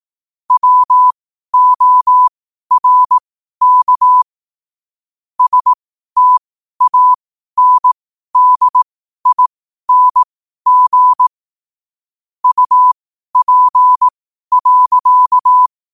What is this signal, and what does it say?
Morse code, heard as a keyed tone, about 18 wpm, spaced as for 12 wpm, 1000 Hz. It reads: WORK STANDING UP.
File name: qotdfiles / 12wpm